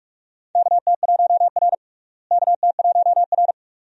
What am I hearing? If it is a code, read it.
Morse code, heard as a keyed tone, about 44 wpm, 700 Hz. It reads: KT1R KT1R
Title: USA Call Signs - 4 Characters - Repeated 2x - 45wpm